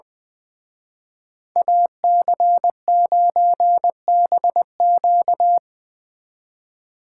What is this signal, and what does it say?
Morse code, heard as a keyed tone, about 23 wpm, 700 Hz. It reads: AC9BQ
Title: Speed Racing - USA Call Signs 20wpm